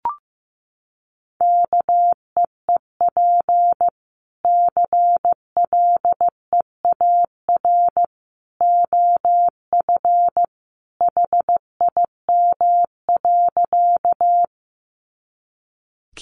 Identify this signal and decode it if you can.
Morse code, heard as a keyed tone, about 15 wpm, 700 Hz. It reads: KEEP CLEAR OF HIM.
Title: Sentences from Top 500 Words 15wpm